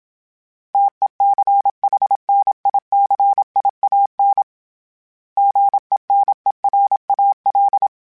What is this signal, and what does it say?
Morse code, heard as a keyed tone, about 27 wpm, 800 Hz. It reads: TECHNICIAN GENERAL
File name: fromSMs / FromSMs mp3s / Classes@25